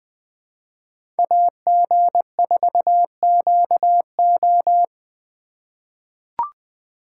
Morse code, heard as a keyed tone, about 21 wpm, 700 Hz. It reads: AG4QO
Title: USA Call Signs - 5 Characters - 4x Character Spacing - 20wpm